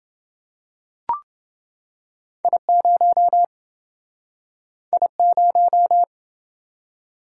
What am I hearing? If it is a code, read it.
Morse code, heard as a keyed tone, about 27 wpm, 700 Hz. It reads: I0 I0